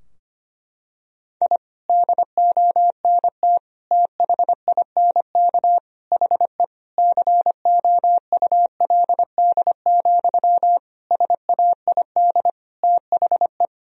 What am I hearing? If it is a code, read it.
Morse code, heard as a keyed tone, about 25 wpm, 700 Hz. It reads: I DONT THINK HE COULD, SAID THE
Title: Sentences from Top 1000 Words - Review Part 4 - 25wpm